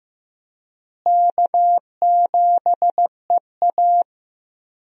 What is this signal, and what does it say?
Morse code, heard as a keyed tone, about 15 wpm, 700 Hz. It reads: K7EA
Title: USA Call Signs - 4 Characters - Repeated 3x - 15wpm